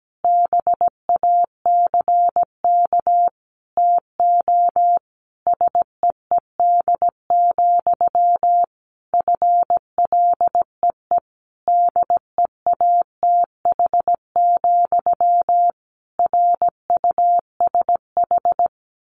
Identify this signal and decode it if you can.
Morse code, heard as a keyed tone, about 17 wpm, 700 Hz. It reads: BACK TO SEED, FLEE DEATH, RUSH
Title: A Sound Of Thunder 17wpm